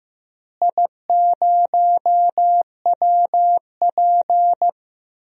Morse code, heard as a keyed tone, about 15 wpm, 700 Hz. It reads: I0WP
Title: DXCC-100 Call Signs - 4 Characters - Repeated 3x - 15wpm